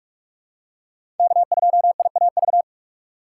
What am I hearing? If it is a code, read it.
Morse code, heard as a keyed tone, about 44 wpm, 690 Hz. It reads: K2IAV
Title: USA Call Signs - 5 Characters - 8x Character Spacing - 45wpm